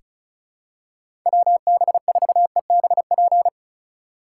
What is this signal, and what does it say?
Morse code, heard as a keyed tone, about 34 wpm, 700 Hz. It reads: WB4EBP
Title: USA Call Signs - 2x Character Spacing - 35wpm